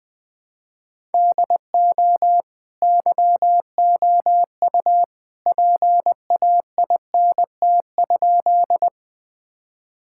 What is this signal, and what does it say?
Morse code, heard as a keyed tone, about 20 wpm, 700 Hz. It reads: DO YOU PAINT?
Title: Sentences from Top 1000 Words - Review Part 6 - 20wpm